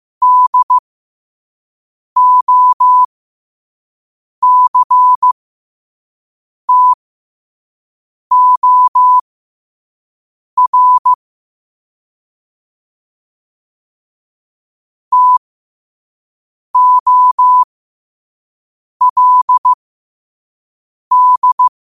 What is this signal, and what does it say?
Morse code, heard as a keyed tone, about 15 wpm, spaced as for 5 wpm, 1000 Hz. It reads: DOCTOR TOLD
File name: qotdfiles / 5wpm